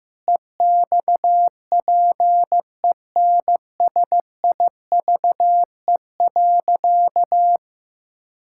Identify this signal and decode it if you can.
Morse code, heard as a keyed tone, about 15 wpm, 700 Hz. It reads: EXPENSIVE.
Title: Pangram Challenge v2 - 15wpm